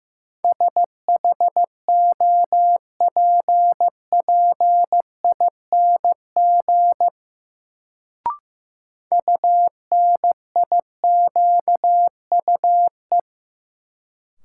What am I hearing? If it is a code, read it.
Morse code, heard as a keyed tone, about 15 wpm, 700 Hz. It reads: SHOPPING UNIQUE
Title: Sentences from Top 1000 Words - Review Part 7 - New Word Drill - 15wpm